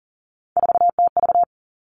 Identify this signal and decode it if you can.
Morse code, heard as a keyed tone, about 38 wpm, 700 Hz. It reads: 4TV
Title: Three Challenging Letter-Number Combinations 40wpm